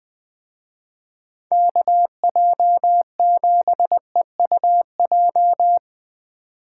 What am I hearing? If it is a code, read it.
Morse code, heard as a keyed tone, about 22 wpm, 700 Hz. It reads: KJ7EUJ
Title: Speed Racing - USA 2x3 Call Signs 20wpm